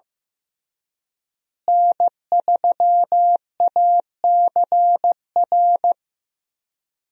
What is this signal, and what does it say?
Morse code, heard as a keyed tone, about 15 wpm, 700 Hz. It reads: N3ACR
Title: USA Call Signs - 5 Characters - Repeated 3x - 15wpm